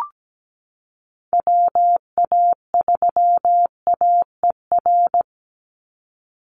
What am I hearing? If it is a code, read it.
Morse code, heard as a keyed tone, about 17 wpm, 700 Hz. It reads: WA3AER
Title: Call Signs - 17wpm